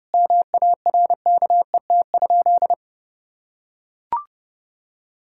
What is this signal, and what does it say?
Morse code, heard as a keyed tone, about 30 wpm, 700 Hz. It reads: MARKET?
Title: Sentences from Top 1000 Words - Review Part 5 - 30wpm